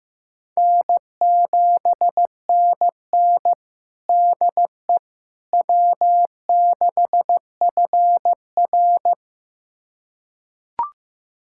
Morse code, heard as a keyed tone, about 15 wpm, 700 Hz. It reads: N7NN DE W6FR
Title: USA Callsign DE Callsign - 4 Characters - 15wpm